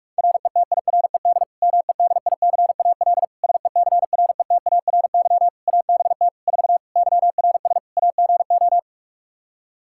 Morse code, heard as a keyed tone, about 46 wpm, 700 Hz. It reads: RETIRED MEDICAL SECRETARY ABT 4 YRS AGO
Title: Practice QSO - Exchange 3 - More Info - 45wpm